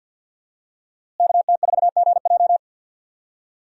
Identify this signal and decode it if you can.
Morse code, heard as a keyed tone, about 52 wpm, 700 Hz. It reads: KT4GJ
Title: USA Call Signs - 5 Characters - Repeated 2x - 50wpm